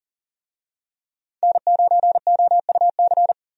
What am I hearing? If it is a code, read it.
Morse code, heard as a keyed tone, about 40 wpm, 700 Hz. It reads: N9OUC
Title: USA Call Signs - 5 Characters - 40wpm